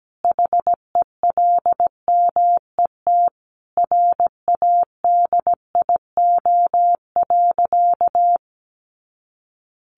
Morse code, heard as a keyed tone, about 17 wpm, 700 Hz. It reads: HELMET RADIO.
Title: A Sound Of Thunder 17wpm